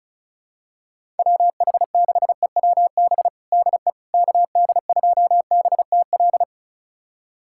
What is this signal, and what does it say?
Morse code, heard as a keyed tone, about 36 wpm, 700 Hz. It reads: WH6EWB DE KD2BTL
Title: USA Callsign DE Callsign - 6 Characters - 35wpm